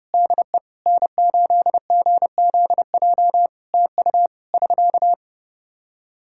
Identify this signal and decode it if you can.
Morse code, heard as a keyed tone, about 32 wpm, 700 Hz. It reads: DE N8GZJ TU <SK>
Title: Practice QSO - Exchange 4 - Ending - 30wpm